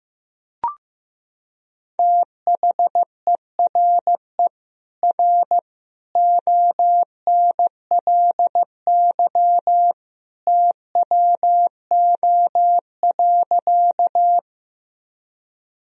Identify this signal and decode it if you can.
Morse code, heard as a keyed tone, about 15 wpm, 700 Hz. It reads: THERE R ONLY TWO.
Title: Sentences from Top 200 Words and Abbreviations 15wpm